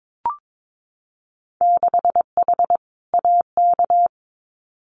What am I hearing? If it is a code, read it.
Morse code, heard as a keyed tone, about 22 wpm, 700 Hz. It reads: 6H AK
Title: Winter Field Day Exchanges Encore 22wpm